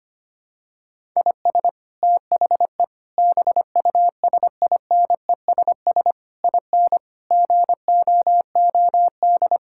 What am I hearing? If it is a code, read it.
Morse code, heard as a keyed tone, about 25 wpm, 700 Hz. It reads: IS THE BUSINESS IN GOOD
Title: Sentences from Top 1000 Words - Review Part 5 - 25wpm